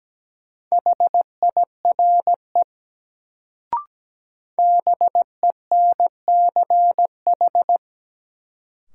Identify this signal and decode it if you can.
Morse code, heard as a keyed tone, about 17 wpm, 700 Hz. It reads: HIRE BENCH
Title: Sentences from Top 1400 Words - New Word Drill - 17wpm